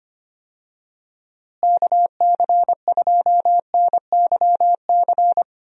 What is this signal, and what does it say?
Morse code, heard as a keyed tone, about 25 wpm, 700 Hz. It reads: KC2NYC
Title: USA Call Signs - 6 Characters - Repeated 3x - 25wpm